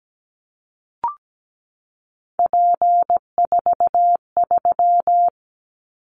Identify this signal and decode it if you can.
Morse code, heard as a keyed tone, about 17 wpm, 700 Hz. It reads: P43
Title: Callsign Prefixes for Top 100 DXCC - 17wpm